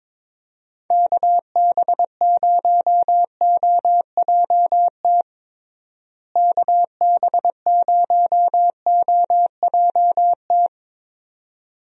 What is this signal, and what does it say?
Morse code, heard as a keyed tone, about 22 wpm, 700 Hz. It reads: KB0OJT KB0OJT
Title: USA Call Signs - 6 Characters - Repeated 2x - 22wpm